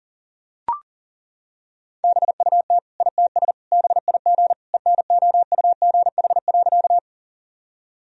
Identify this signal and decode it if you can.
Morse code, heard as a keyed tone, about 40 wpm, 700 Hz. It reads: BUT ITS BIG ENOUGH.
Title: Sentences from Top 1000 Words - Review Part 1 - 40wpm